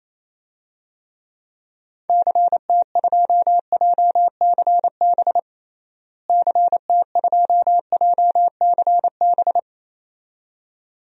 Morse code, heard as a keyed tone, about 27 wpm, 700 Hz. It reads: CT2JCB CT2JCB
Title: DXCC-100 Call Signs - 6 Characters - Repeated 2x - 28wpm